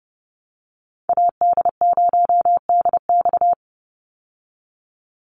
Morse code, heard as a keyed tone, about 30 wpm, 700 Hz. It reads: AD0DX
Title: Call Signs Repeated 2x - 30wpm